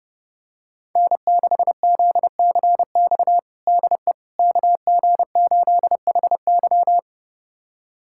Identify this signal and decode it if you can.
Morse code, heard as a keyed tone, about 30 wpm, 700 Hz. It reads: N6ZCX DE KG8HY
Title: USA Callsign DE Callsign 30wpm